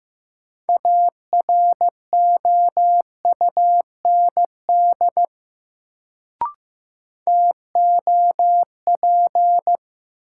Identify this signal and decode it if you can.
Morse code, heard as a keyed tone, about 15 wpm, 700 Hz. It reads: AROUND TOP
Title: Sets of 2 Words - New Word Drill - 15wpm